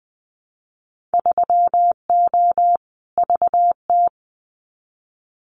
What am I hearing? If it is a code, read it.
Morse code, heard as a keyed tone, about 20 wpm, 700 Hz. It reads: 3O VT
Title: Winter Field Day Exchanges Encore 20wpm